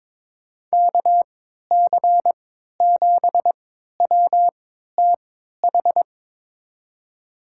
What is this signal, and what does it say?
Morse code, heard as a keyed tone, about 22 wpm, spaced as for 12 wpm, 700 Hz. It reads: KC7WTH
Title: USA Call Signs - 3x Character Spacing - 22wpm